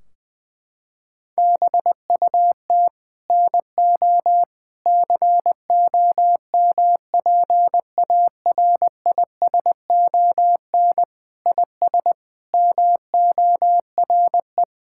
Morse code, heard as a keyed tone, about 20 wpm, 700 Hz. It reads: BUT NO COMPARISON IS MORE
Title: Sentences from Top 1600 Words 20wpm